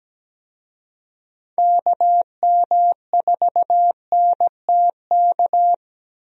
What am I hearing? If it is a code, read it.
Morse code, heard as a keyed tone, about 17 wpm, 700 Hz. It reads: KM4NTK
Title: USA Call Signs - 6 Characters - Repeated 2x - 17wpm